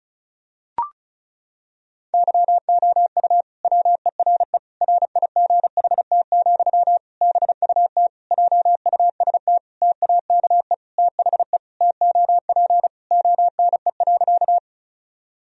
Morse code, heard as a keyed tone, about 34 wpm, 700 Hz. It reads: YOU WERE RIGHT, BUT JUST TAKE THE TOP ONE.
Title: Sentences from Top 1000 Words - Review Part 2 - 35wpm